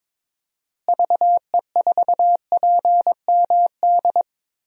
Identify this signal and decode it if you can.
Morse code, heard as a keyed tone, about 21 wpm, 700 Hz. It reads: VE4PMD
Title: DXCC-100 Call Signs - 6 Characters - Repeated 3x - 22wpm